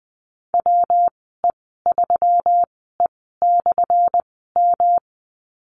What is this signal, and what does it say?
Morse code, heard as a keyed tone, about 20 wpm, spaced as for 14 wpm, 700 Hz. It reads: WE3E/M
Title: 179 Lesson 29g - Slash - Call Signs 20wpm